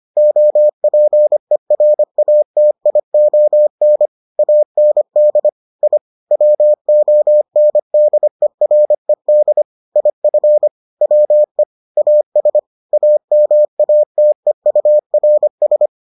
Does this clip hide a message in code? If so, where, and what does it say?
Yes: Morse code from 0.2 to 15.9 s, heard as a keyed tone, about 25 wpm, 600 Hz. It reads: OPERATION AND I WONDERED IF WE AS AMATEURS